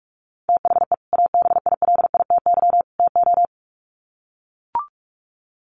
Morse code, heard as a keyed tone, about 44 wpm, 700 Hz. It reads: THE ABILITY TO
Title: Sets of 3 Words 45wpm